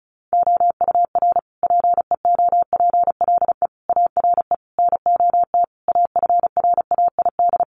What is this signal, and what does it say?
Morse code, heard as a keyed tone, about 34 wpm, 700 Hz. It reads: OUR PEOPLE ARE NOT AFRAID
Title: Sentences from Top 1500 Words 35wpm